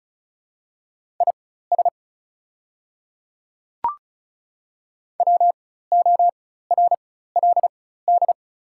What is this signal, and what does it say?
Morse code, heard as a keyed tone, about 34 wpm, spaced as for 16 wpm, 690 Hz. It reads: IS WORLD
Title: 4x Character Spacing - Top 200 Words - 35wpm